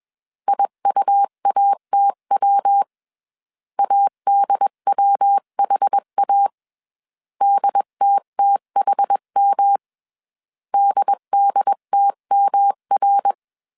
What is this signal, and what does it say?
Morse code, heard as a keyed tone, about 21 wpm, 800 Hz. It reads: IUATW ADWHA DTTHM DDTMR
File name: 20wpm